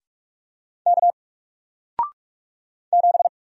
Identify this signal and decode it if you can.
Morse code, heard as a keyed tone, about 44 wpm, 700 Hz. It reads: K 7
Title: Single Character (Letter:Number:Punctuation:Prosign) Alt 45wpm